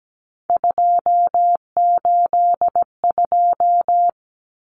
Morse code, heard as a keyed tone, about 17 wpm, 700 Hz. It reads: 282
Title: Numbers - 3-Digit 17wpm